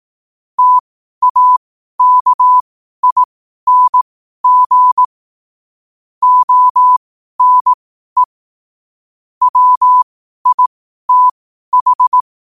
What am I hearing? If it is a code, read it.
Morse code, heard as a keyed tone, about 18 wpm, spaced as for 12 wpm, 1000 Hz. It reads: TAKING ONE WITH
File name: qotdfiles / 12wpm